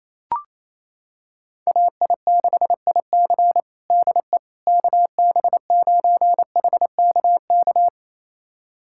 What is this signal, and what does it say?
Morse code, heard as a keyed tone, about 28 wpm, 700 Hz. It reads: AI6IC DE KB9HKK
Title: USA Callsign DE Callsign 28wpm